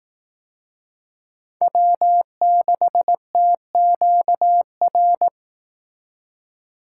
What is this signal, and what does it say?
Morse code, heard as a keyed tone, about 18 wpm, 700 Hz. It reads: W6TQR
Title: Speed Racing - USA 1x3 Call Signs 15wpm